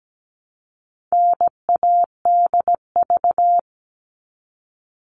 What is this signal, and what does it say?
Morse code, heard as a keyed tone, about 17 wpm, 700 Hz. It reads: NADV
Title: All Cut Numbers - 4-Digit 17wpm